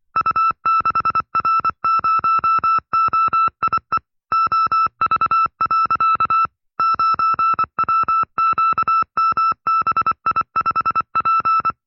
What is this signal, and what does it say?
Morse code, heard as a keyed tone, about 25 wpm, 1310 Hz. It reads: U6R0OIE OV. 9WQMBI5P
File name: FreeFiles / CW-practice / 20WPM-side2